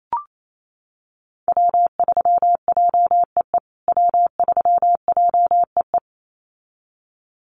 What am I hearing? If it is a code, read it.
Morse code, heard as a keyed tone, about 28 wpm, 700 Hz. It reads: W3JEE W3JEE
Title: USA Call Signs Repeated 2x 28wpm